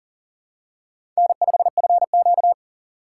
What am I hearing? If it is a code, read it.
Morse code, heard as a keyed tone, about 42 wpm, 700 Hz. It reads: N5FQ
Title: USA Call Signs - 4 Characters - 2x Character Spacing - 40wpm